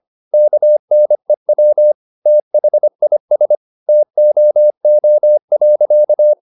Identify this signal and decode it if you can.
Morse code, heard as a keyed tone, about 25 wpm, 600 Hz. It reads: KNEW THIS TOO.
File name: the old man and the sea part 1 25wpm